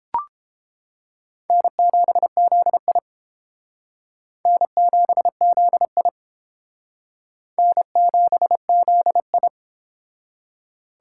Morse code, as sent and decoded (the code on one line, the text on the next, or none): -. --... --.. .. / -. --... --.. .. / -. --... --.. ..
N7ZI N7ZI N7ZI